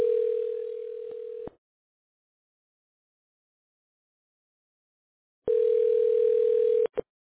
Ringback tone (North America), 440 + 480 Hz.